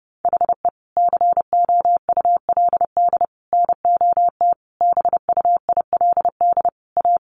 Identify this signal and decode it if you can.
Morse code, heard as a keyed tone, about 30 wpm, 700 Hz. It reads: HE COULD NOT BUILD A
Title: Sentences from Top 1100 Words 30wpm